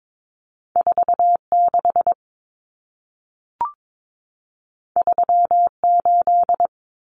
Morse code, heard as a keyed tone, about 22 wpm, 700 Hz. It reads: 46 38
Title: Numbers - 2-Digit 22wpm